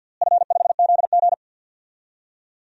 Morse code, heard as a keyed tone, about 50 wpm, 700 Hz. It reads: F5ZG
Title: DXCC-100 Call Signs - 4 Characters - Repeated 2x - 50wpm